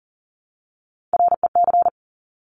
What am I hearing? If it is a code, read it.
Morse code, heard as a keyed tone, about 41 wpm, 690 Hz. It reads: REC
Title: Top Three Letter Combinations in Words 40wpm